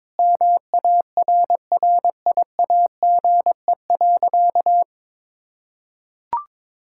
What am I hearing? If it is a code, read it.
Morse code, heard as a keyed tone, about 22 wpm, 700 Hz. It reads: MARRIAGE.
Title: Sentences from Top 1000 Words - Review Part 5 - 22wpm